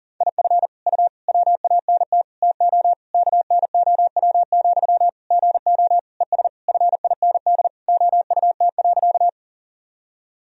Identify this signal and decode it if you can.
Morse code, heard as a keyed tone, about 40 wpm, 700 Hz. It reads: IF U WANT TO KNOW, GO ES FIND OUT.
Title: Sentences from Top 200 Words and Abbreviations 40wpm